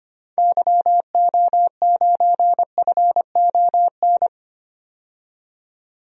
Morse code, heard as a keyed tone, about 25 wpm, 700 Hz. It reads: YO9FON
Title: DXCC-100 Call Signs - 6 Characters - 25wpm